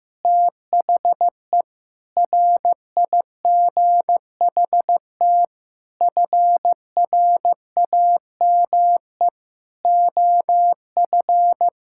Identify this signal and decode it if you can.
Morse code, heard as a keyed tone, about 15 wpm, 700 Hz. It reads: THE RIGHT FRAME OF
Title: Sentences from Top 1000 Words - Review Part 6 - 15wpm